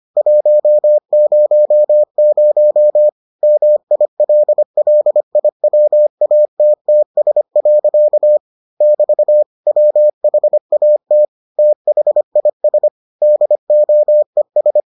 Morse code, heard as a keyed tone, about 25 wpm, 600 Hz. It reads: 100 MILLIWATTS. <BT> WHAT THIS DOES